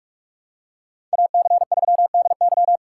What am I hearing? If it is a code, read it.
Morse code, heard as a keyed tone, about 44 wpm, 700 Hz. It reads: AC3DY